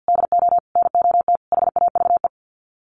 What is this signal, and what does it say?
Morse code, heard as a keyed tone, about 50 wpm, 700 Hz. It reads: DO NOT HAVE
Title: Sets of 5 Words Encore 50wpm